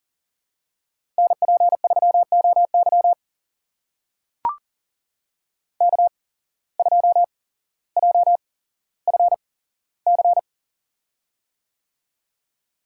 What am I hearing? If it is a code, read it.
Morse code, heard as a keyed tone, about 40 wpm, spaced as for 17 wpm, 700 Hz. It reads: NP3OY K2JFC
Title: USA Call Signs - 5 Characters - 8x Character Spacing - 40wpm